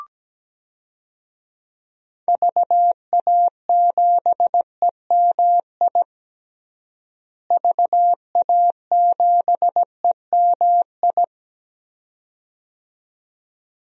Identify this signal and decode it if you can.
Morse code, heard as a keyed tone, about 17 wpm, 700 Hz. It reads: VA7EMI VA7EMI
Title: DXCC-100 Call Signs - 6 Characters - Repeated 2x - 17wpm